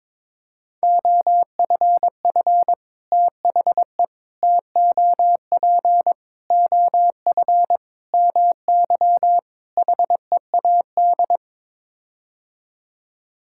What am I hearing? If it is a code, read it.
Morse code, heard as a keyed tone, about 22 wpm, 700 Hz. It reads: OFF THE TOP OF MY HEAD
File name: practice / 22wpm / Sets of 6 Words - 22wpm